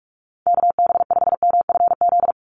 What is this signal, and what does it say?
Morse code, heard as a keyed tone, about 44 wpm, 700 Hz. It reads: KB5MFZ
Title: USA Call Signs 45wpm